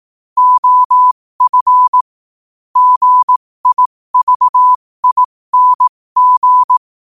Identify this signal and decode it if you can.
Morse code, heard as a keyed tone, about 18 wpm, spaced as for 15 wpm, 1000 Hz. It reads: OF GIVING
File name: qotdfiles / 15wpm